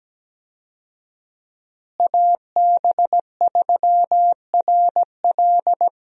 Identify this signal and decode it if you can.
Morse code, heard as a keyed tone, about 17 wpm, 700 Hz. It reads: AB3RL
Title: USA Call Signs - 5 Characters - Repeated 3x - 17wpm